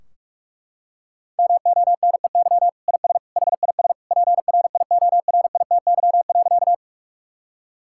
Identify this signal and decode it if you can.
Morse code, heard as a keyed tone, about 41 wpm, 700 Hz. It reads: MONEY IS HIS PRIORITY.
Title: Sentences from Top 1600 Words 45wpm